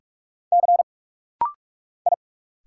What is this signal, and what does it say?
Morse code, heard as a keyed tone, about 44 wpm, 700 Hz. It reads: C I
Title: Single Letter-Number Alt - Rapid-Fire 45wpm